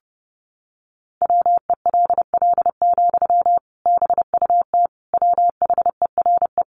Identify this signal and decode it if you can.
Morse code, heard as a keyed tone, about 28 wpm, 700 Hz. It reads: WELL, BUT WHERE
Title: Sentences from Top 500 Words 30wpm